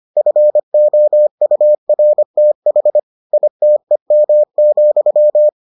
Morse code, heard as a keyed tone, about 25 wpm, 600 Hz. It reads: FOURTH ITEM,